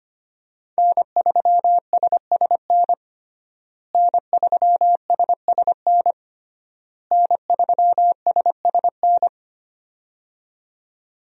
Morse code, heard as a keyed tone, about 25 wpm, 700 Hz. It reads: N3SSN N3SSN N3SSN
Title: USA Call Signs - 5 Characters - Repeated 3x - 25wpm